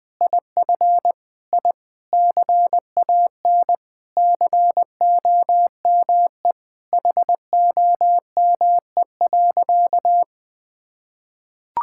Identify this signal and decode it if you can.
Morse code, heard as a keyed tone, about 20 wpm, 700 Hz. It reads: IF I CAN COME HOME.
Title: Sentences with 5-Word Phrases - 20wpm